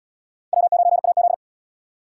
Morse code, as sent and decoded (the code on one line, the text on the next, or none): .-- ----. .- --..
W9AZ